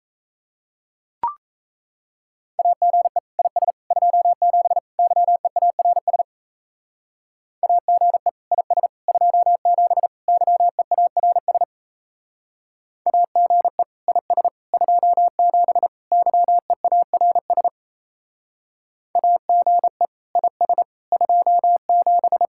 Morse code, heard as a keyed tone, about 33 wpm, 700 Hz. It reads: AGE IS 27 YEARS AGE IS 27 YEARS AGE IS 27 YEARS AGE IS 27
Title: Speed-Racing Common QSO Phrases - 28wpm